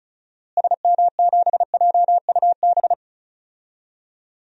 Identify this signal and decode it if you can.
Morse code, heard as a keyed tone, about 34 wpm, 700 Hz. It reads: SM7JUB
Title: DXCC-100 Call Signs - 6 Characters - Repeated 2x - 35wpm